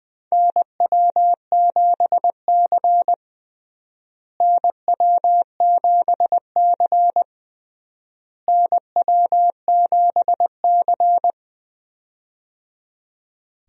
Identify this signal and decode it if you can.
Morse code, heard as a keyed tone, about 20 wpm, 700 Hz. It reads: NW7C NW7C NW7C
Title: USA Call Signs - 4 Characters - Repeated 3x - 20wpm